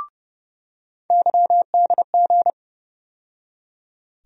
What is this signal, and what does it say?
Morse code, heard as a keyed tone, about 30 wpm, 700 Hz. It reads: YDG
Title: USA Call Sign Postfixes - 3 Letters 30wpm